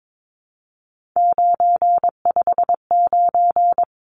Morse code, heard as a keyed tone, about 22 wpm, 700 Hz. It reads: 959
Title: Numbers - 3-Digit 22wpm